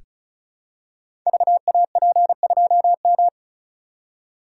Morse code, heard as a keyed tone, about 36 wpm, 700 Hz. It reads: VAP2M
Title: Code Groups - 5 Letter:Numbers - 35wpm